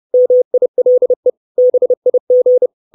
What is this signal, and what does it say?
Morse code, heard as a keyed tone, about 30 wpm, 500 Hz. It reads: MILE BIG
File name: 632 Words 30WPM